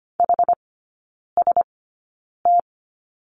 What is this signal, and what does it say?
Morse code, heard as a keyed tone, about 25 wpm, spaced as for 9 wpm, 700 Hz. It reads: HST
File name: course / 25:10wpm / 093 Lesson 16e - Letter c - Three Characters 25wpm